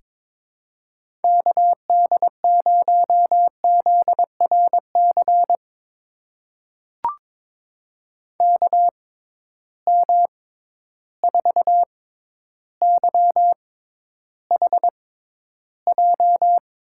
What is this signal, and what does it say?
Morse code, heard as a keyed tone, about 23 wpm, spaced as for 11 wpm, 700 Hz. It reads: KD0ZRC KM4YHJ
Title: USA Call Signs - 6 Characters - 6x Character Spacing - 22wpm